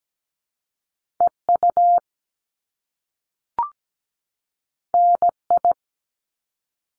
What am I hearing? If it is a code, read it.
Morse code, heard as a keyed tone, about 17 wpm, 700 Hz. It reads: EU NI
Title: Top Two Letter Combinations in Words 17wpm